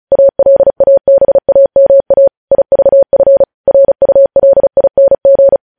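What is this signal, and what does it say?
Morse code, heard as a keyed tone, about 36 wpm, 550 Hz. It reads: ALABAMA IVF RULING
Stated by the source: Morse code news headlines practice sent at 35 wpm.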